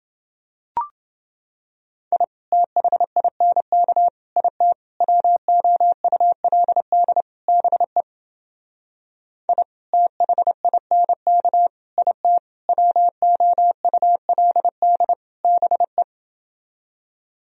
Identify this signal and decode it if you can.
Morse code, heard as a keyed tone, about 29 wpm, 700 Hz. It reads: I THINK IT WOULD BE I THINK IT WOULD BE